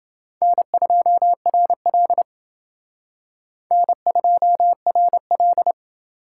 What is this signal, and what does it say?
Morse code, heard as a keyed tone, about 28 wpm, 690 Hz. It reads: N2RL N2RL